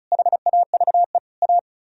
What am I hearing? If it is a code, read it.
Morse code, heard as a keyed tone, about 43 wpm, 700 Hz. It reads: HAVE A
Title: Speed-Racing - Sets of 3 Words 35wpm